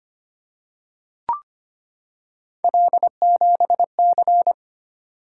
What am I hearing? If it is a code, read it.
Morse code, heard as a keyed tone, about 25 wpm, 700 Hz. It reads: L7C
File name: Three Letter-Number Combinations - 25wpm